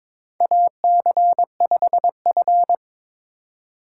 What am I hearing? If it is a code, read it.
Morse code, heard as a keyed tone, about 22 wpm, 700 Hz. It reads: AC5F
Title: USA Call Signs - 4 Characters - 4x Character Spacing - 22wpm